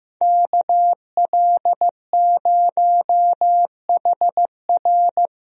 Morse code, heard as a keyed tone, about 15 wpm, 700 Hz. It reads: KL0HR
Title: USA Call Signs - 5 Characters - Repeated 3x - 15wpm